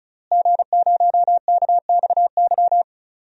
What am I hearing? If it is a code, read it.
Morse code, heard as a keyed tone, about 36 wpm, 700 Hz. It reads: G0KXY
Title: DXCC-100 Call Signs - 5 Characters - Repeated 2x - 35wpm